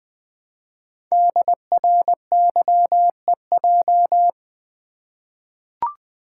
Morse code, heard as a keyed tone, about 20 wpm, 700 Hz. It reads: DRYEJ
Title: Code Groups - 5 Letters - 20wpm